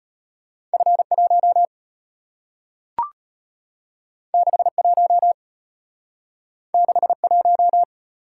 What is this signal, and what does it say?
Morse code, heard as a keyed tone, about 34 wpm, 700 Hz. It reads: F1 61 61